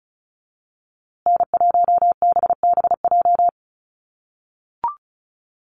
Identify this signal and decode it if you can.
Morse code, heard as a keyed tone, about 34 wpm, 700 Hz. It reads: N1BBJ
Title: USA Call Signs Repeated 3x 35wpm